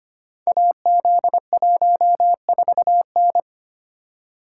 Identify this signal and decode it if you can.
Morse code, heard as a keyed tone, about 25 wpm, 700 Hz. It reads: AZ14N
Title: Code Groups - 5 Letter:Numbers - 25wpm